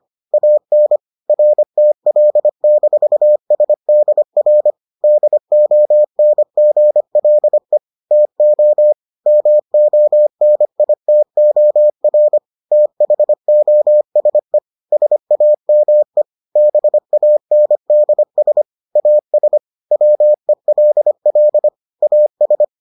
Morse code, heard as a keyed tone, about 25 wpm, 600 Hz. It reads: AN RTL-SDR DONGLE TO MONITOR THOSE SAME BANDS AS WELL AS